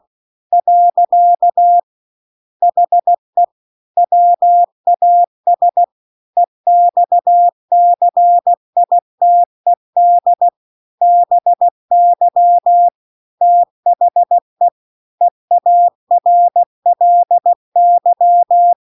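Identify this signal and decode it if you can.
Morse code, heard as a keyed tone, about 16 wpm, 700 Hz. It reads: . HE WAS EXCITED BY THE EARLY
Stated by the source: Morse practice text at 16 WPM.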